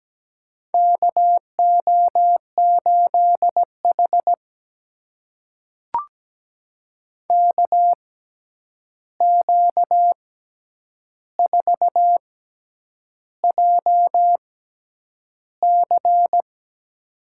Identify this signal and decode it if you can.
Morse code, heard as a keyed tone, about 17 wpm, spaced as for 8 wpm, 700 Hz. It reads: KO8H KQ4JC
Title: USA Call Signs - 6x Character Spacing - 17wpm